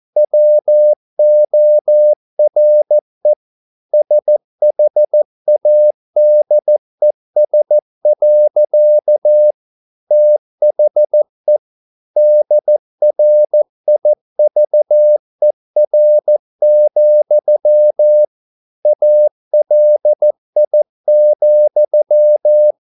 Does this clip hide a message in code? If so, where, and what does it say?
Yes: Morse code from 0.2 to 22.7 s, heard as a keyed tone, about 14 wpm, 600 Hz. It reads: WORE SHADES. THE DRIVER, ALI,